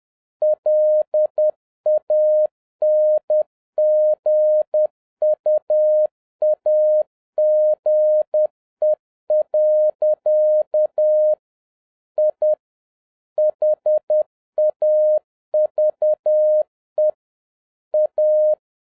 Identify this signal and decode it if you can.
Morse code, heard as a keyed tone, about 10 wpm, 610 Hz. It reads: LANGUAGE. I HAVE A
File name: 2011-07-mp3 / n21 GREETINGS FROM SINGAPORE 10wpm